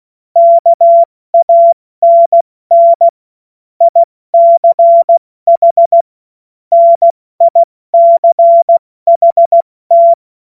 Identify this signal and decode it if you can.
Morse code, heard as a keyed tone, about 16 wpm, 680 Hz. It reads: KANN ICH NICHT
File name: Morsekurs / Witze / Witz 13. Dezember 2022 70 F80